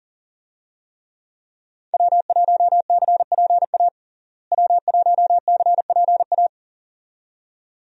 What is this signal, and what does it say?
Morse code, heard as a keyed tone, about 40 wpm, 700 Hz. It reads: W1CPA W1CPA
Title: USA Call Signs - 5 Characters - Repeated 2x - 40wpm